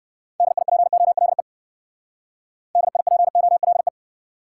Morse code, heard as a keyed tone, about 56 wpm, 700 Hz. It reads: DIPOLE DIPOLE
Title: Speed-Racing Top 90 QSO Elements 45wpm